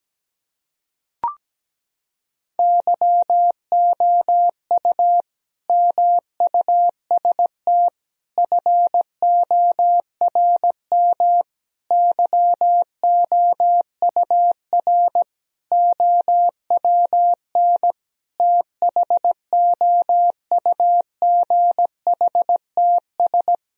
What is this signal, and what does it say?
Morse code, heard as a keyed tone, about 17 wpm, 700 Hz. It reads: YOU MUST FORM YOUR OWN THOUGHTS
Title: Sentences from Top 1000 Words - Review Part 1 - 17wpm